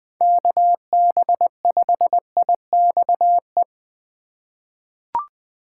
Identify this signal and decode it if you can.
Morse code, heard as a keyed tone, about 20 wpm, 700 Hz. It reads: KB5IXE
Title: USA Call Signs - 6 Characters - 4x Character Spacing - 20wpm